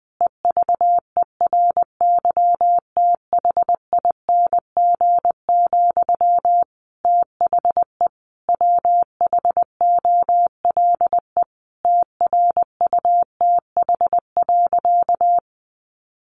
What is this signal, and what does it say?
Morse code, heard as a keyed tone, about 20 wpm, 700 Hz. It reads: EVERYTHING, THE WHOLE TRUTH.
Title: Sentences from Top 800 Words 20wpm